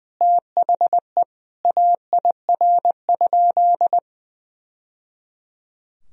Morse code, heard as a keyed tone, about 20 wpm, 700 Hz. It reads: THE AIR?
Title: Sentences from Top 1000 Words - Review Part 2 - 20wpm